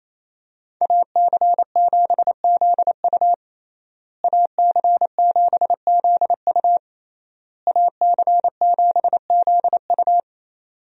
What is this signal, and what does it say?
Morse code, heard as a keyed tone, about 27 wpm, 700 Hz. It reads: AC7ZU AC7ZU AC7ZU
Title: USA Call Signs - 5 Characters - Repeated 3x - 28wpm